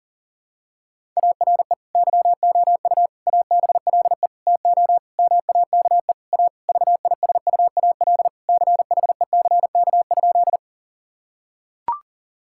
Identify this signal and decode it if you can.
Morse code, heard as a keyed tone, about 40 wpm, 700 Hz. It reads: ARE YOU ABLE TO MAKE A VISUAL CHECK?
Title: Sentences from Top 1000 Words - Review Part 6 - 40wpm